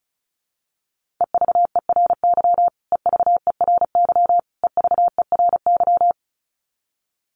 Morse code, heard as a keyed tone, about 34 wpm, 700 Hz. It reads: EVERY EVERY EVERY
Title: Top 200 Words Repeated 35wpm